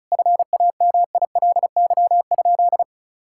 Morse code, heard as a keyed tone, about 36 wpm, 690 Hz. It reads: FAMILY?
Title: Sentences from Top 1000 Words - Review Part 5 - 35wpm